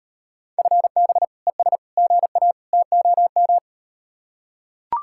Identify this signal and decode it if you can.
Morse code, heard as a keyed tone, about 38 wpm, 700 Hz. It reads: FB ES GA TOM